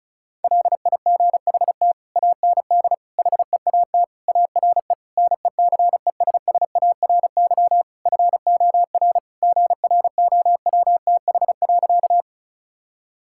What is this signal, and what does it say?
Morse code, heard as a keyed tone, about 36 wpm, 700 Hz. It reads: LIGHT AND HEAT ARE NECESSARY FOR GROWTH.
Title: Sentences from Top 1000 Words - Review Part 5 - 35wpm